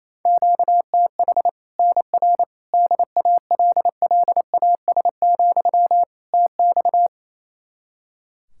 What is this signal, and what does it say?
Morse code, heard as a keyed tone, about 28 wpm, 700 Hz. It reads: QTH NR DALLAS, TX
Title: Practice QSO - Exchange 1 - Beginning - 28wpm